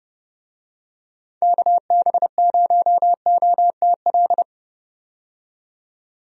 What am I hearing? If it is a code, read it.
Morse code, heard as a keyed tone, about 32 wpm, 700 Hz. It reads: KB0OTL
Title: USA Call Signs - 6 Characters - 30wpm